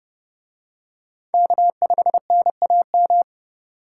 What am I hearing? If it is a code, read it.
Morse code, heard as a keyed tone, about 29 wpm, 700 Hz. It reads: K5NAM